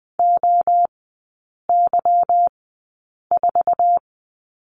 Morse code, heard as a keyed tone, about 20 wpm, spaced as for 8 wpm, 700 Hz. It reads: OY4
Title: 172 Lesson 28e - Number 9 - Three Characters 20wpm